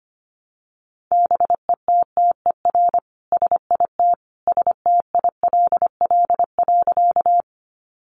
Morse code, heard as a keyed tone, about 25 wpm, 700 Hz. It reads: BETTER SIT STILL.
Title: Sentences from Top 500 Words 25wpm